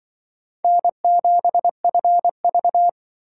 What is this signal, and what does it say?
Morse code, heard as a keyed tone, about 24 wpm, 700 Hz. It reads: N7FV
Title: Speed Racing - USA 1x2 Call Signs 20wpm